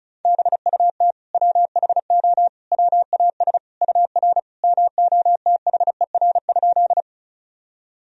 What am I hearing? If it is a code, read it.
Morse code, heard as a keyed tone, about 36 wpm, 700 Hz. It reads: BUT WHO WAS UR MOTHER?
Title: Sentences from Top 200 Words and Abbreviations 35wpm